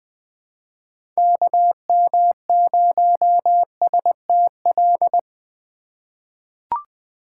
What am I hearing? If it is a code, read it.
Morse code, heard as a keyed tone, about 20 wpm, 700 Hz. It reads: KM0STL
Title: USA Call Signs - 6 Characters - 2x Character Spacing - 20wpm